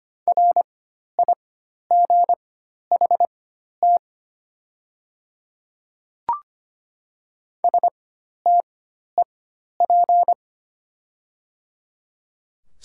Morse code, heard as a keyed tone, about 25 wpm, spaced as for 12 wpm, 690 Hz. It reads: RIGHT STEP
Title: 4x Character Spacing - Top 500 Words - 25wpm